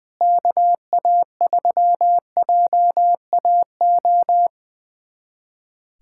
Morse code, heard as a keyed tone, about 20 wpm, 700 Hz. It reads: KA3JAO
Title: USA Callsign DE Callsign 20wpm